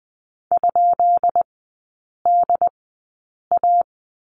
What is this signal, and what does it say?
Morse code, heard as a keyed tone, about 20 wpm, spaced as for 8 wpm, 690 Hz. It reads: ?DA
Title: 191 Lesson 31e - Letter v - Three Characters 20wpm